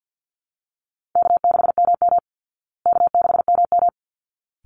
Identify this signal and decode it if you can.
Morse code, heard as a keyed tone, about 49 wpm, 700 Hz. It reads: K6MM K6MM
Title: Call Signs Repeated 3x - 50wpm